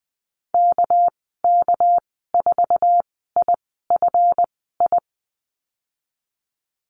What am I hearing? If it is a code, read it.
Morse code, heard as a keyed tone, about 19 wpm, spaced as for 14 wpm, 700 Hz. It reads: KK4IFI
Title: 231 Lesson 37g - Letter x - Call Signs 20wpm